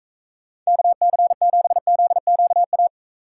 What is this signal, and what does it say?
Morse code, heard as a keyed tone, about 42 wpm, 700 Hz. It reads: KC7ZQA